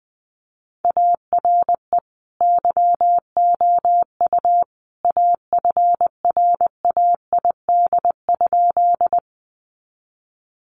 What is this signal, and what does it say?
Morse code, heard as a keyed tone, about 20 wpm, 700 Hz. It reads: ARE YOU AFRAID?
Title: Sentences from Top 1500 Words 20wpm